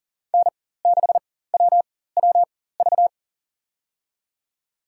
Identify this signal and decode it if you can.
Morse code, heard as a keyed tone, about 40 wpm, spaced as for 19 wpm, 700 Hz. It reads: N6WWV
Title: USA Call Signs - 5 Characters - 4x Character Spacing - 40wpm